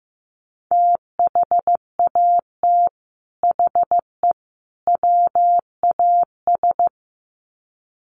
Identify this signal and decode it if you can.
Morse code, heard as a keyed tone, about 15 wpm, 700 Hz. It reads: THAT HE WAS
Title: Sets of 3 Words 15wpm